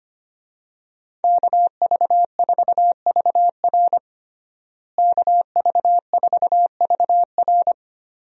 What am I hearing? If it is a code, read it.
Morse code, heard as a keyed tone, about 25 wpm, 700 Hz. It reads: KV4VR KV4VR
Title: USA Call Signs - 5 Characters - Repeated 3x - 25wpm